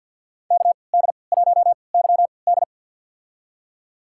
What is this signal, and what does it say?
Morse code, heard as a keyed tone, about 47 wpm, spaced as for 27 wpm, 700 Hz. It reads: KD1YD
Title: USA Call Signs - 5 Characters - 3x Character Spacing - 50wpm